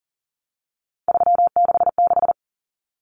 Morse code, heard as a keyed tone, about 40 wpm, 700 Hz. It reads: 366
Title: Common Cut Numbers - 3-Digit 40wpm